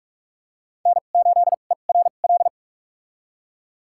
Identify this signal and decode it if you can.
Morse code, heard as a keyed tone, about 46 wpm, spaced as for 33 wpm, 700 Hz. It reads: N8ERL
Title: USA Call Signs - 5 Characters - 2x Character Spacing - 45wpm